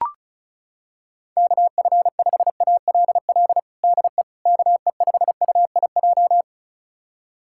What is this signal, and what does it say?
Morse code, heard as a keyed tone, about 36 wpm, 700 Hz. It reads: KF5ALL DE KE5UIJ
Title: USA Callsign DE Callsign 35wpm